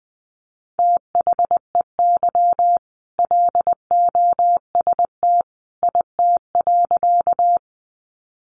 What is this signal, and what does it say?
Morse code, heard as a keyed tone, about 20 wpm, 700 Hz. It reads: THEY LOST IT.
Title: Sentences from Top 1400 Words 20wpm